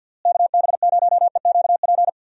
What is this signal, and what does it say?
Morse code, heard as a keyed tone, about 52 wpm, 700 Hz. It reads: KB0EQP
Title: USA Call Signs - 6 Characters - Repeated 2x - 50wpm